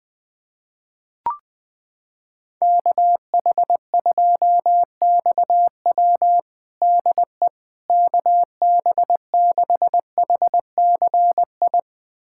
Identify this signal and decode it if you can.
Morse code, heard as a keyed tone, about 20 wpm, 700 Hz. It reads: KH2XW DE KB6HCI
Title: USA Callsign DE Callsign 20wpm